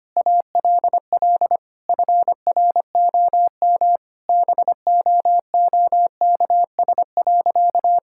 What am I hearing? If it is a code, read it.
Morse code, heard as a keyed tone, about 25 wpm, 700 Hz. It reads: ALL FROM BOOKS.
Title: Sentences from Top 1000 Words - Review Part 6 - 25wpm